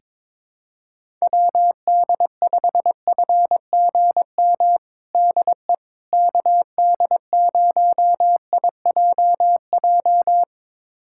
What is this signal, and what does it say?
Morse code, heard as a keyed tone, about 22 wpm, 700 Hz. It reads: WD5FGM DE KD0IJJ
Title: USA Callsign DE Callsign - 6 Characters - 22wpm